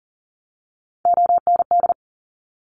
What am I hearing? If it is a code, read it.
Morse code, heard as a keyed tone, about 38 wpm, 690 Hz. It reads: OND
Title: Top Three Letter Combinations in Words 40wpm